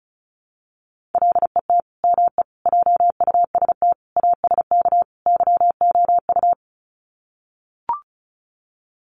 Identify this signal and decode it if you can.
Morse code, heard as a keyed tone, about 34 wpm, 700 Hz. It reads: LET ME JUST ASK YOU
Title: Sets of 5 Words Encore2 35wpm